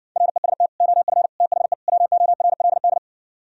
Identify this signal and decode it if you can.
Morse code, heard as a keyed tone, about 59 wpm, 700 Hz. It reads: REST OF THE WORLD